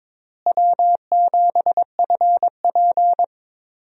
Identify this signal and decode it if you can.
Morse code, heard as a keyed tone, about 22 wpm, 700 Hz. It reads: W7FP
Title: USA Call Signs - 4 Characters - Repeated 3x - 22wpm